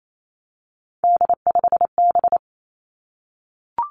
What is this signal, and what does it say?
Morse code, heard as a keyed tone, about 28 wpm, 700 Hz. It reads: D5B
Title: Three Challenging Letter-Number Combinations 28wpm